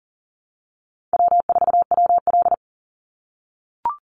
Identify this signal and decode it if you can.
Morse code, heard as a keyed tone, about 40 wpm, 700 Hz. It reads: W4WL
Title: USA Call Signs 40wpm